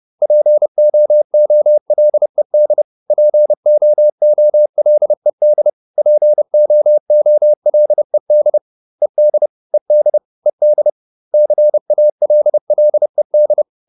Morse code, heard as a keyed tone, about 30 wpm, 600 Hz. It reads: POOLED POOLED POOLED ED ED ED CALLED